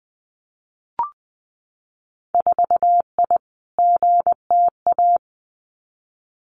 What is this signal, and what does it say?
Morse code, heard as a keyed tone, about 20 wpm, 700 Hz. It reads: 4I GTA
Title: Winter Field Day Exchanges Encore 20wpm